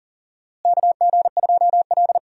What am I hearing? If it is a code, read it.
Morse code, heard as a keyed tone, about 40 wpm, 700 Hz. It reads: KG2L
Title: USA Call Signs - 4 Characters - Repeated 2x - 40wpm